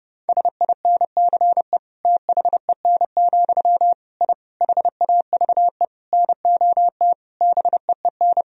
Morse code, heard as a keyed tone, about 30 wpm, 700 Hz. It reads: SINCE THEN, I HAVE NOT BEEN
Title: Sentences from Top 1000 Words - Review Part 1 - 30wpm